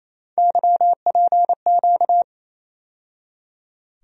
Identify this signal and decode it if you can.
Morse code, heard as a keyed tone, about 28 wpm, 700 Hz. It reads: YPQ
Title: USA Call Sign Postfixes - 3 Letters 28wpm